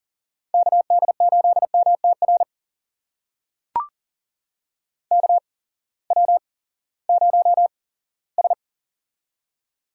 Morse code, heard as a keyed tone, about 40 wpm, spaced as for 20 wpm, 700 Hz. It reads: KD8MTR KW0S
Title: USA Call Signs - 8x Character Spacing - 40wpm